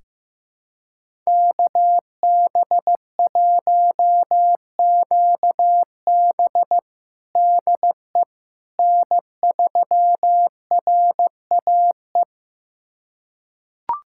Morse code, heard as a keyed tone, about 15 wpm, 700 Hz. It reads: KB1QB DE N3RAE
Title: USA Callsign DE Callsign - 5 Characters - 15wpm